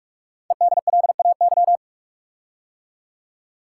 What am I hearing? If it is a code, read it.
Morse code, heard as a keyed tone, about 44 wpm, 700 Hz. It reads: EDLAY
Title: Code Groups - 5 Letters - 45wpm